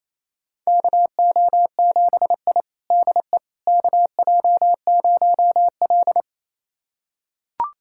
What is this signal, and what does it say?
Morse code, heard as a keyed tone, about 28 wpm, 700 Hz. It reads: KO7I DE KJ0L
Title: USA Callsign DE Callsign - 4 Characters - 28wpm